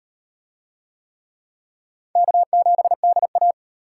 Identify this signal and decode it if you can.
Morse code, heard as a keyed tone, about 37 wpm, 700 Hz. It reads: K7DA